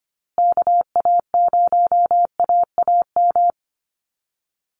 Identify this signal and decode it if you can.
Morse code, heard as a keyed tone, about 25 wpm, 700 Hz. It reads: KA0AAM
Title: Call Signs Repeated 2x - 25wpm